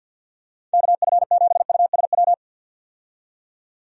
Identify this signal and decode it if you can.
Morse code, heard as a keyed tone, about 46 wpm, 700 Hz. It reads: KF7USW
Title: USA Call Signs - 6 Characters - 50wpm